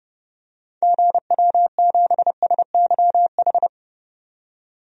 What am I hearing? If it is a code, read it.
Morse code, heard as a keyed tone, about 30 wpm, 700 Hz. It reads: GW7SYH
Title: DXCC-100 Call Signs - 6 Characters - 30wpm